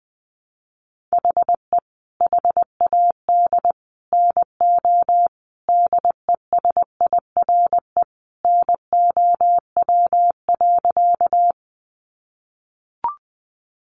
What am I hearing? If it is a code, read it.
Morse code, heard as a keyed tone, about 20 wpm, 700 Hz. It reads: HE HAD NO DESIRE NOW.
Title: Sentences from Top 1100 Words 20wpm